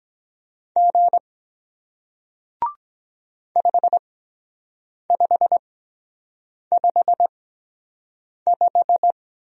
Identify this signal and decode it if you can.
Morse code, heard as a keyed tone, about 23 wpm, spaced as for 7 wpm, 700 Hz. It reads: G 5555